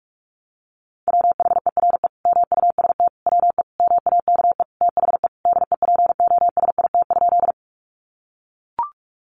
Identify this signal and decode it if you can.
Morse code, heard as a keyed tone, about 44 wpm, 700 Hz. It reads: WHERE MUST WE MAKE THE DEPOSIT?
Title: Sentences from Top 1300 Words 45wpm